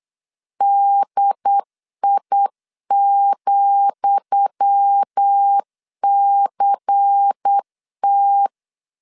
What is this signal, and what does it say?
Morse code, heard as a keyed tone, about 8 wpm, 800 Hz. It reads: DI,CT
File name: LEZIONI / 10 SIGNS / 10D COMA 8wpm 3749